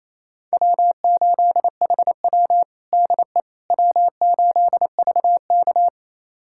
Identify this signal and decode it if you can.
Morse code, heard as a keyed tone, about 28 wpm, 700 Hz. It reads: W8HW DE W8VK
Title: USA Callsign DE Callsign - 4 Characters - 28wpm